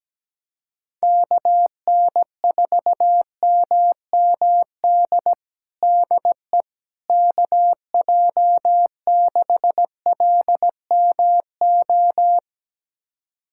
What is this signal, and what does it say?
Morse code, heard as a keyed tone, about 17 wpm, 700 Hz. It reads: KN4MMD DE KJ6LMO
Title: USA Callsign DE Callsign - 6 Characters - 17wpm